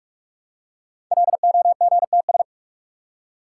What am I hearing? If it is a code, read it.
Morse code, heard as a keyed tone, about 44 wpm, 700 Hz. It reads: LOGTS
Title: Code Groups - 5 Letter:Numbers - 45wpm